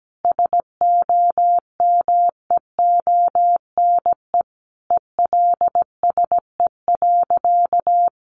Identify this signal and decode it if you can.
Morse code, heard as a keyed tone, about 17 wpm, 700 Hz. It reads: SOMEONE ELSE.
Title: Sentences from Top 800 Words 17wpm